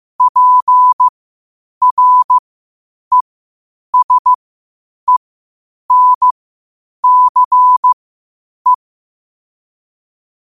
Morse code, heard as a keyed tone, about 15 wpm, spaced as for 8 wpm, 1000 Hz. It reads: PRESENCE
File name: qotdfiles / 8wpm